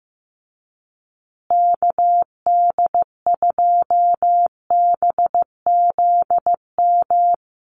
Morse code, heard as a keyed tone, about 14 wpm, 700 Hz. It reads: KD2BZM
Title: USA Call Signs Repeated 2x 15wpm